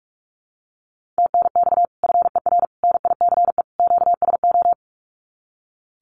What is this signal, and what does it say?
Morse code, heard as a keyed tone, about 44 wpm, 700 Hz. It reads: TNX FER NICE QSO
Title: Common QSO Phrases 45wpm